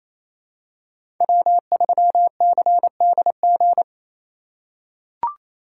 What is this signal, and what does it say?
Morse code, heard as a keyed tone, about 28 wpm, 700 Hz. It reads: W3CDG
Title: USA Call Signs - 5 Characters - 4x Character Spacing - 28wpm